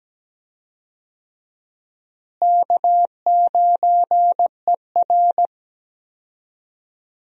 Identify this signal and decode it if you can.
Morse code, heard as a keyed tone, about 17 wpm, 700 Hz. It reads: K9ER
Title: USA Call Signs - 4 Characters - Repeated 3x - 17wpm